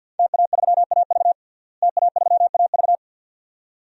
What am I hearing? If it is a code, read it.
Morse code, heard as a keyed tone, about 51 wpm, 700 Hz. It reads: TA3AV TA3AV
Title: DXCC-100 Call Signs - 5 Characters - Repeated 2x - 50wpm